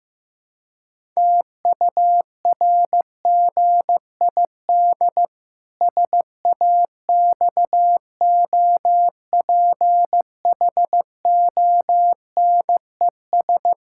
Morse code, heard as a keyed tone, about 15 wpm, 700 Hz. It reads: TURGID SAXOPHONES
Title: Pangram Challenge v2 - 15wpm